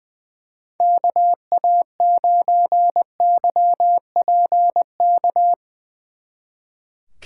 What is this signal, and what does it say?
Morse code, heard as a keyed tone, about 21 wpm, 700 Hz. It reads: KA9YPK
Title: Speed Racing - USA 2x3 Call Signs 20wpm